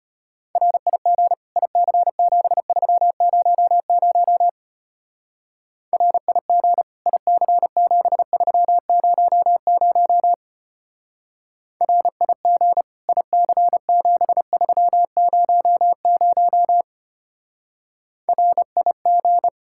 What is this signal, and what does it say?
Morse code, heard as a keyed tone, about 30 wpm, 700 Hz. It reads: RIG IC7300 RIG IC7300 RIG IC7300 RIG
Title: Speed-Racing Common QSO Phrases - 25wpm